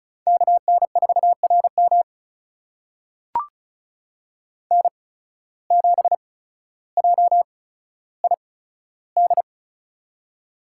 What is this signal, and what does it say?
Morse code, heard as a keyed tone, about 36 wpm, spaced as for 15 wpm, 700 Hz. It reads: KN4RM N7JID
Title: USA Call Signs - 5 Characters - 8x Character Spacing - 35wpm